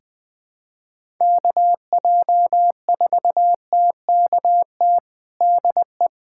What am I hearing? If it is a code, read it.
Morse code, heard as a keyed tone, about 20 wpm, 700 Hz. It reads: KJ4TKT DE
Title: USA Callsign DE Callsign 20wpm